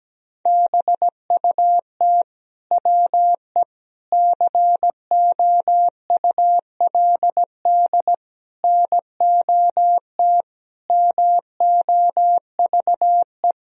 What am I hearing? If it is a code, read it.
Morse code, heard as a keyed tone, about 17 wpm, 700 Hz. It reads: BUT WE COULD NOT MOVE
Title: Sentences from Top 1000 Words - Review Part 2 - 17wpm